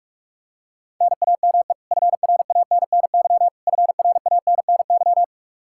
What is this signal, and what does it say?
Morse code, heard as a keyed tone, about 42 wpm, 700 Hz. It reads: NAME FRANNY FRANNY
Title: Practice QSO - Exchange 1 - Beginning - 45wpm